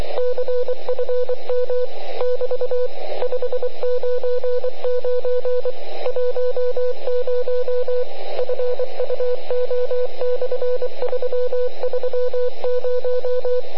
Morse code, heard as a keyed tone, about 26 wpm, 500 Hz, with radio noise. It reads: CFM <BT> 599 10 FUO/330
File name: ARCHIV / QRP2005 / ok1wsl